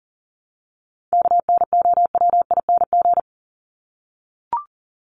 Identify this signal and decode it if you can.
Morse code, heard as a keyed tone, about 40 wpm, 700 Hz. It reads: KNOWING
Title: ING Suffix - Top 100 Words with Base Words 40wpm